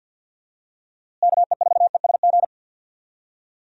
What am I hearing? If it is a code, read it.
Morse code, heard as a keyed tone, about 53 wpm, 700 Hz. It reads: KE4ESG
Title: USA Call Signs - 6 Characters - Repeated 3x - 50wpm